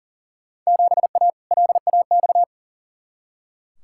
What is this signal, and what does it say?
Morse code, heard as a keyed tone, about 40 wpm, 700 Hz. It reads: 7A LAX
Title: Field Day Exchanges Encore - 40wpm